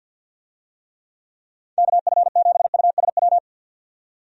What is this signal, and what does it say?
Morse code, heard as a keyed tone, about 46 wpm, 700 Hz. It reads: KF7USW
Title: USA Call Signs - 6 Characters - 50wpm